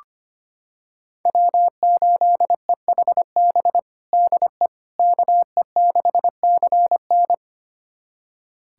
Morse code, heard as a keyed tone, about 25 wpm, 700 Hz. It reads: W8EHB DE KE6CN
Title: USA Callsign DE Callsign - 5 Characters - 25wpm